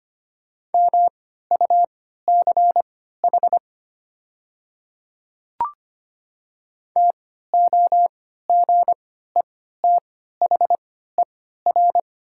Morse code, heard as a keyed tone, about 25 wpm, spaced as for 14 wpm, 700 Hz. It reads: MUCH TOGETHER
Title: 3x Character Spacing - Top 500 Words - 25wpm